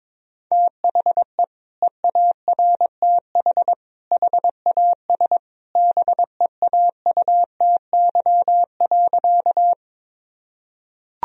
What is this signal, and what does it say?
Morse code, heard as a keyed tone, about 22 wpm, 700 Hz. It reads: THE EARTH HAS BEAUTY.
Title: Sentences from Top 1000 Words - Review Part 2 - 22wpm